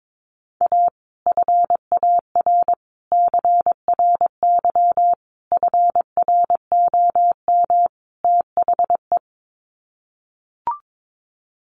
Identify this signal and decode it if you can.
Morse code, heard as a keyed tone, about 22 wpm, 700 Hz. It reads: A FAR CRY FROM THE
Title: Sets of 5 Words Encore 22wpm